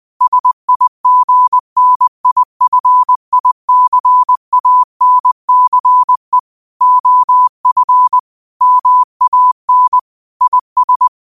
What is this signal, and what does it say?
Morse code, heard as a keyed tone, about 20 wpm, 1000 Hz. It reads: SIGNIFICANCE OF MAN IS